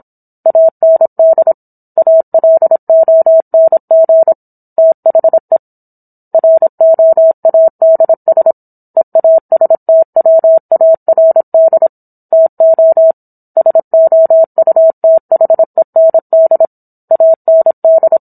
Morse code, heard as a keyed tone, about 26 wpm, 650 Hz. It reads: AND ALONG THE ROADS EASTWARD TO SOUTHEND AND